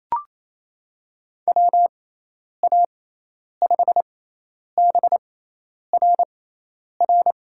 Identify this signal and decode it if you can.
Morse code, heard as a keyed tone, about 27 wpm, spaced as for 10 wpm, 700 Hz. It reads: WA5BRR
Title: USA Call Signs - 6 Characters - 6x Character Spacing - 28wpm